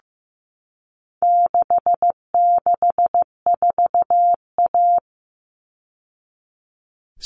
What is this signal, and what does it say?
Morse code, heard as a keyed tone, about 15 wpm, 700 Hz. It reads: 664A
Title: All Cut Numbers - 4-Digit 15wpm